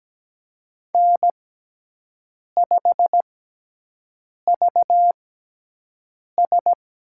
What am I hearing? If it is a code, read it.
Morse code, heard as a keyed tone, about 17 wpm, spaced as for 6 wpm, 700 Hz. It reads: N5VS
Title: USA Call Signs - 5 Characters - 6x Character Spacing - 17wpm